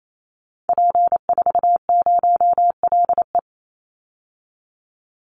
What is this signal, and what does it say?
Morse code, heard as a keyed tone, about 28 wpm, 700 Hz. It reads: P40LE
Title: Call Signs - 28wpm